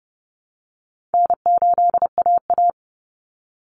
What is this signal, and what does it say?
Morse code, heard as a keyed tone, about 30 wpm, 700 Hz. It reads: N8AA
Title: Call Signs Repeated 2x - 30wpm